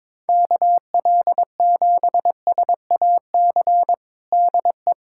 Morse code, heard as a keyed tone, about 22 wpm, 700 Hz. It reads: KL7SAC DE
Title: USA Callsign DE Callsign - 6 Characters - 22wpm